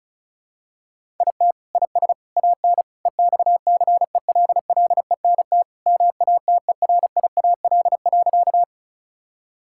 Morse code, heard as a keyed tone, about 34 wpm, 700 Hz. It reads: IT IS AN EXCELLENT MATERIAL.
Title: Sentences from Top 1000 Words - Review Part 5 - 35wpm